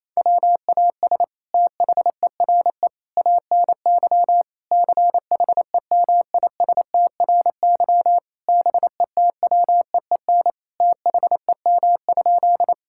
Morse code, heard as a keyed tone, about 28 wpm, 700 Hz. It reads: WAS THERE ANY CHEMISTRY BETWEEN THEM?
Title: Sentences from Top 1000 Words - Review Part 6 - 28wpm